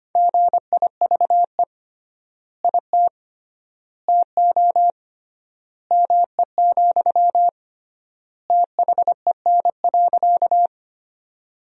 Morse code, heard as a keyed tone, about 25 wpm, 700 Hz. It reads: GIVE IT TO ME, THEN.
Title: 3x Word-Spacing - Sentences from Top 200 Words - 25wpm